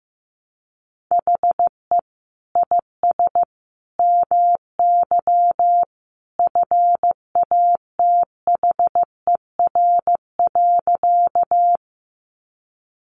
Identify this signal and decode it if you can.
Morse code, heard as a keyed tone, about 15 wpm, 700 Hz. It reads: HE IS MY FATHER.
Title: Sentences from Top 300 Words 15wpm